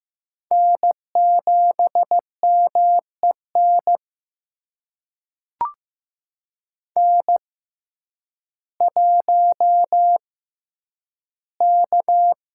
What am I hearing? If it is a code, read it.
Morse code, heard as a keyed tone, about 15 wpm, spaced as for 9 wpm, 700 Hz. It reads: N7MEN N1K
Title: USA Call Signs - 5 Characters - 6x Character Spacing - 15wpm